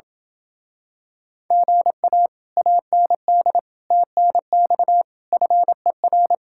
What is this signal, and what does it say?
Morse code, heard as a keyed tone, about 27 wpm, 700 Hz. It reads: GA AND TNX FER